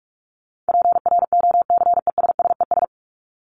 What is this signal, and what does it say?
Morse code, heard as a keyed tone, about 44 wpm, 700 Hz. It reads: PROCESSES
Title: ES:S Suffix - Top 100 Words with Base Words 45wpm